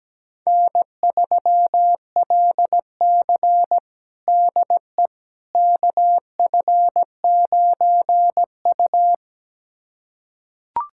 Morse code, heard as a keyed tone, about 17 wpm, 700 Hz. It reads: N3LC DE KF9U
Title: USA Callsign DE Callsign - 4 Characters - 17wpm